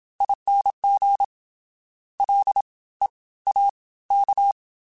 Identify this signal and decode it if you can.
Morse code, heard as a keyed tone, about 27 wpm, spaced as for 18 wpm, 800 Hz. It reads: ING LEAK